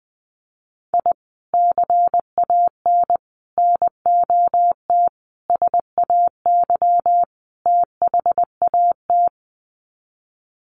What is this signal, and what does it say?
Morse code, heard as a keyed tone, about 20 wpm, 700 Hz. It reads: I CAN NOT SAY THAT
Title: Sets of 5 Words Encore2 20wpm